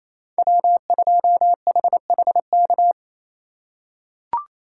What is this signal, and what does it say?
Morse code, heard as a keyed tone, about 28 wpm, 700 Hz. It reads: W2HHK
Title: USA Call Signs - 5 Characters - 2x Character Spacing - 28wpm